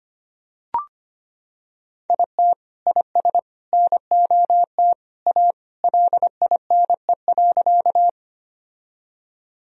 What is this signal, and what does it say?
Morse code, heard as a keyed tone, about 25 wpm, 700 Hz. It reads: IT IS NOT A LINE.
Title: Sentences from Top 1000 Words - Review Part 1 - 25wpm